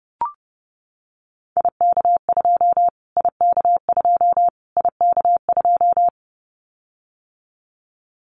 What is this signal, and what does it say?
Morse code, heard as a keyed tone, about 30 wpm, 700 Hz. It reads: IK2 IK2 IK2
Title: Callsign Prefixes for Top 25 DXCC - Repeated 3x - 30wpm